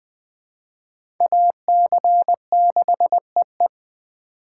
Morse code, heard as a keyed tone, about 20 wpm, 700 Hz. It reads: AC6EE
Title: USA Call Signs - 5 Characters - 2x Character Spacing - 20wpm